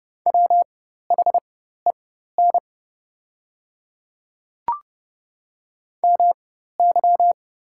Morse code, heard as a keyed tone, about 30 wpm, spaced as for 14 wpm, 690 Hz. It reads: WHEN MY
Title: 4x Character Spacing - Top 200 Words - 30wpm